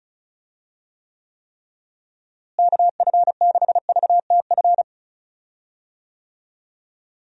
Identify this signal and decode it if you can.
Morse code, heard as a keyed tone, about 34 wpm, 700 Hz. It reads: KF6VTF
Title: USA Call Signs - 6 Characters - Repeated 3x - 35wpm